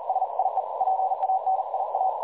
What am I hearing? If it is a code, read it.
Morse code, heard as a keyed tone, about 41 wpm, 760 Hz, with noise. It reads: SIE PANIA
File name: dlazaawansowanych / nagraniazpasma / websdr recording 3565.0kHz